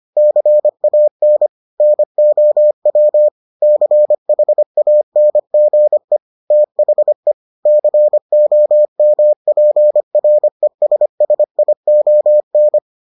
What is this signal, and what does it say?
Morse code, heard as a keyed tone, about 25 wpm, 600 Hz. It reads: CAN NOW CHANGE THE COMPRESSION